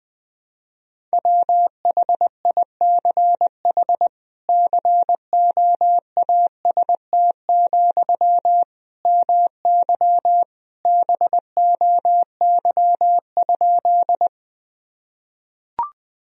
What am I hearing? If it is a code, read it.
Morse code, heard as a keyed tone, about 20 wpm, 700 Hz. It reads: WHICH COAST, MY BOY?
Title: Sentences from Top 1000 Words - Review Part 4 - 20wpm